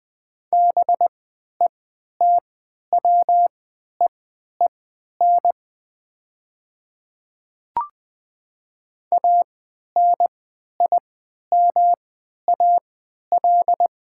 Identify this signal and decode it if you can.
Morse code, heard as a keyed tone, about 21 wpm, spaced as for 12 wpm, 700 Hz. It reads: BETWEEN ANIMAL
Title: 3x Character Spacing - Top 300 Words - 20wpm